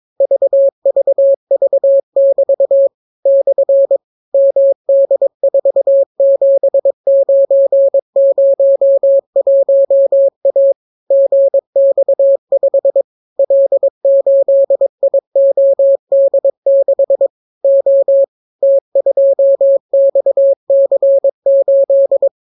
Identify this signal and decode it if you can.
Morse code, heard as a keyed tone, about 22 wpm, 550 Hz. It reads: VVV<BT> / MD47901A GX5 L8IOD6 O T2XC8